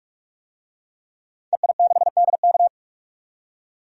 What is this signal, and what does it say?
Morse code, heard as a keyed tone, about 46 wpm, 700 Hz. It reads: EI6DK
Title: DXCC-100 Call Signs - 5 Characters - 45wpm